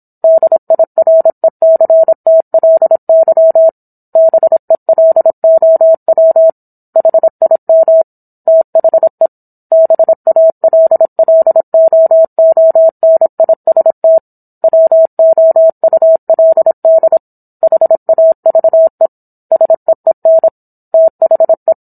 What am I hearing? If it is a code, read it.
Morse code, heard as a keyed tone, about 27 wpm, 650 Hz. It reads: DIRECTLY BELOW HIM THE BALLOONIST WOULD HAVE SEEN THE